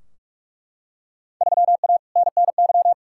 Morse code, heard as a keyed tone, about 43 wpm, 700 Hz. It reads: 3A NNY
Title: Field Day Exchanges - 45wpm